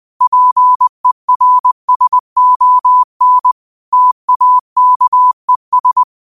Morse code, heard as a keyed tone, about 20 wpm, 1000 Hz. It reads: PERSON TAKES
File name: qotdfiles / 20wpm